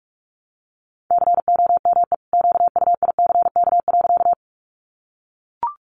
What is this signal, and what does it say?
Morse code, heard as a keyed tone, about 44 wpm, 700 Hz. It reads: COME QUICK.
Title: Sentences from Top 500 Words 45wpm